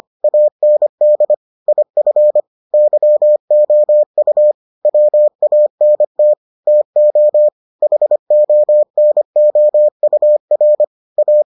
Morse code, heard as a keyed tone, about 25 wpm, 600 Hz. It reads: AND IF YOU WANT TO HONOUR A